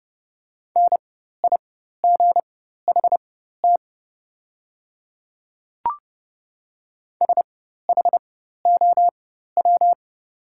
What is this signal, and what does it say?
Morse code, heard as a keyed tone, about 29 wpm, spaced as for 14 wpm, 690 Hz. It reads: NIGHT SHOW
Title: 4x Character Spacing - Top 500 Words - 30wpm